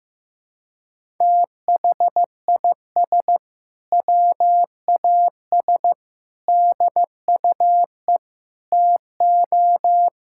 Morse code, heard as a keyed tone, about 15 wpm, 700 Hz. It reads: THIS WAS DUE TO
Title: Sentences from Top 1600 Words 15wpm